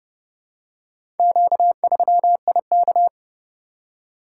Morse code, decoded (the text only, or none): Q3IK